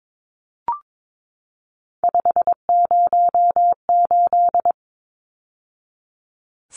Morse code, heard as a keyed tone, about 22 wpm, 700 Hz. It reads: 508
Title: Numbers - 3-Digit 22wpm